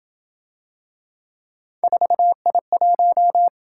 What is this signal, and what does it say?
Morse code, heard as a keyed tone, about 26 wpm, 700 Hz. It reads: 4I1